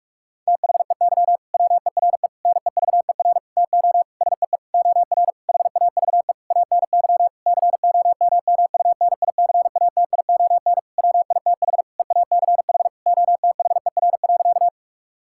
Morse code, heard as a keyed tone, about 42 wpm, 700 Hz. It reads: THEY WERE NEVER TO SEE OR HAVE ANY COMMUNICATION WITH EACH OTHER.
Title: Sentences from Top 1000 Words - Review Part 6 - 45wpm